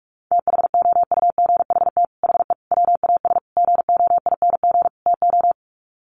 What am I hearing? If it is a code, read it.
Morse code, heard as a keyed tone, about 44 wpm, 700 Hz. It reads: THOUGHT HE WAS GOING TO
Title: Sets of 5 Words Encore2 45wpm